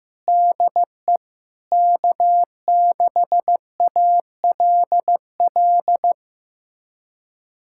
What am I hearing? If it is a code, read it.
Morse code, heard as a keyed tone, about 16 wpm, 700 Hz. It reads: DE K6ALL
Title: USA Callsign DE Callsign 15wpm